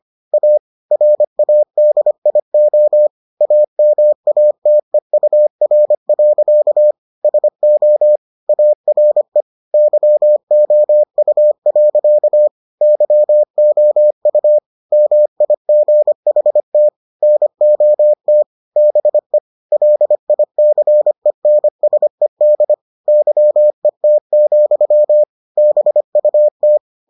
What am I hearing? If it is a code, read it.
Morse code, heard as a keyed tone, about 25 wpm, 600 Hz. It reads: A RADIO AMATEUR. SO ARE YOU. YOU MIGHT NOT BE LICENSED YET, BUT